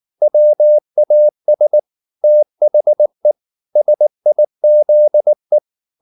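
Morse code, heard as a keyed tone, about 19 wpm, 600 Hz. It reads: WAS THE SIZE
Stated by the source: Morse code at 19 wpm from CW med Gnister, reading Ken Follett's novel Never in English.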